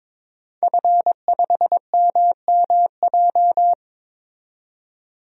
Morse code, heard as a keyed tone, about 22 wpm, 700 Hz. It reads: F5MMJ
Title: DXCC-100 Call Signs - 5 Characters - 22wpm